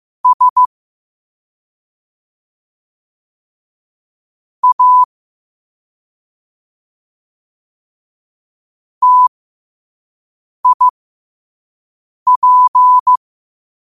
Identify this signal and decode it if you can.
Morse code, heard as a keyed tone, about 15 wpm, spaced as for 5 wpm, 1000 Hz. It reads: S A TIP
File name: qotdfiles / 5wpm